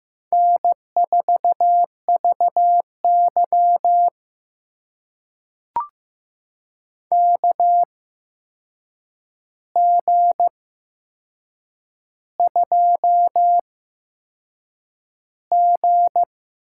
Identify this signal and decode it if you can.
Morse code, heard as a keyed tone, about 15 wpm, spaced as for 6 wpm, 700 Hz. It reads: N4VY KG2G
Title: USA Call Signs - 4 Characters - 8x Character Spacing - 15wpm